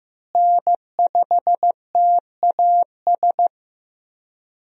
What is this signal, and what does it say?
Morse code, heard as a keyed tone, about 15 wpm, 700 Hz. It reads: N5TAS
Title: USA Call Signs - 5 Characters - Repeated 3x - 15wpm